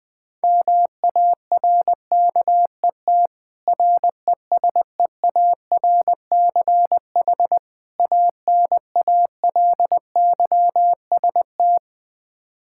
Morse code, heard as a keyed tone, about 20 wpm, 700 Hz. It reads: MARKET RESEARCH ANALYST
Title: Practice QSO - Exchange 3 - More Info - 20wpm